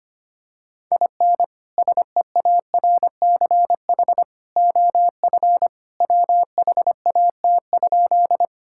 Morse code, heard as a keyed tone, about 25 wpm, 700 Hz. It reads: IN SEARCH OF WHAT?
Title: Sentences from Top 1000 Words - Review Part 5 - 25wpm